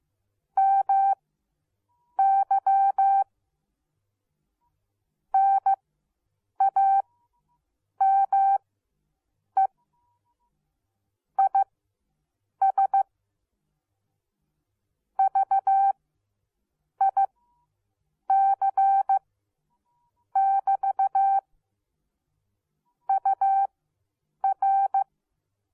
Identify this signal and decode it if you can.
Morse code, heard as a keyed tone, about 15 wpm, spaced as for 7 wpm, 780 Hz. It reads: MY NAME IS VIC <BT> UR